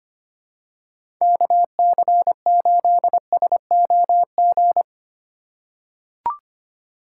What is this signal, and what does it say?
Morse code, heard as a keyed tone, about 25 wpm, 700 Hz. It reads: KC8SOG
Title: USA Call Signs - 6 Characters - 2x Character Spacing - 25wpm